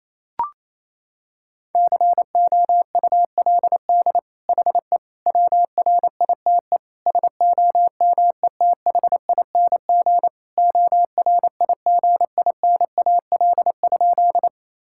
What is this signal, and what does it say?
Morse code, heard as a keyed tone, about 28 wpm, 700 Hz. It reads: COULD HE WRITE SOMETHING ORIGINAL?
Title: Sentences from Top 1000 Words - Review Part 6 - 28wpm